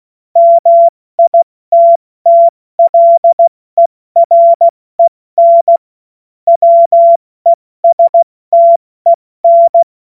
Morse code, heard as a keyed tone, about 16 wpm, 680 Hz. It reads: MITTLEREN WESTEN